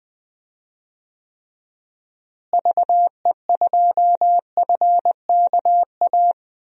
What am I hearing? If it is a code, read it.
Morse code, heard as a keyed tone, about 21 wpm, 700 Hz. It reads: VE2FKA
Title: DXCC-100 Call Signs - 6 Characters - Repeated 3x - 20wpm